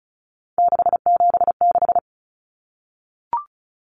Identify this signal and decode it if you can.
Morse code, heard as a keyed tone, about 34 wpm, 700 Hz. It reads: 676
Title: Numbers - 3-Digit 35wpm